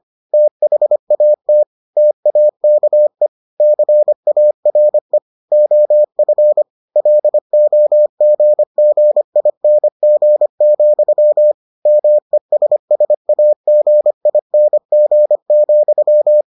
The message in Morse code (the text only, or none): THAT TAKE CARE OF LOGGING, MESSAGING,